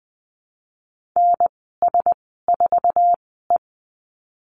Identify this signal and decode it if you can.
Morse code, heard as a keyed tone, about 20 wpm, spaced as for 15 wpm, 700 Hz. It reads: NS4E
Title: 055 Lesson 10g - Letter r - Call Signs 20wpm